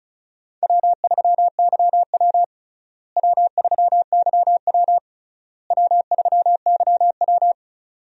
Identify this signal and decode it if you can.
Morse code, heard as a keyed tone, about 36 wpm, 700 Hz. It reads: W3YW W3YW W3YW
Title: USA Call Signs - 4 Characters - Repeated 3x - 35wpm